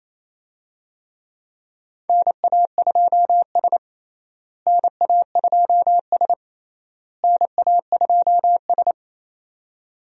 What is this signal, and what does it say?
Morse code, heard as a keyed tone, about 28 wpm, 700 Hz. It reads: NA2S NA2S NA2S
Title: USA Call Signs - 4 Characters - Repeated 3x - 28wpm